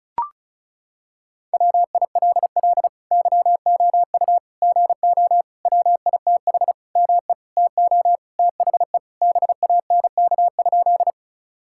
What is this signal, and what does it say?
Morse code, heard as a keyed tone, about 34 wpm, 700 Hz. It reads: WILL YOU GO WITH ME TO THE BANK?
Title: Sentences from Top 1000 Words - Review Part 6 - 35wpm